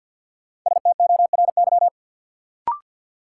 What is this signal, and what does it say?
Morse code, heard as a keyed tone, about 47 wpm, 700 Hz. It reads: STORY
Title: Sentences from Top 300 Words - New Word Drill - 50wpm